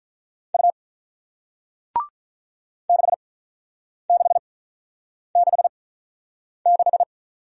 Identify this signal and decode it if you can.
Morse code, heard as a keyed tone, about 47 wpm, spaced as for 9 wpm, 690 Hz. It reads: U 6666